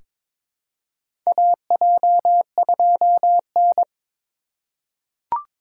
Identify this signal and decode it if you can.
Morse code, heard as a keyed tone, about 22 wpm, 700 Hz. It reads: AJ2N
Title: USA Call Signs - 4 Characters - 3x Character Spacing - 22wpm